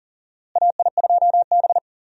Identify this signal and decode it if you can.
Morse code, heard as a keyed tone, about 40 wpm, 700 Hz. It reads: AI2B
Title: USA Call Signs - 4 Characters - Repeated 2x - 40wpm